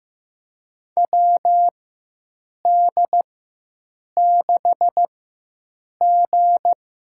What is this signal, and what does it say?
Morse code, heard as a keyed tone, about 15 wpm, spaced as for 7 wpm, 700 Hz. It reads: WD6G
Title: USA Call Signs - 4x Character Spacing - 15wpm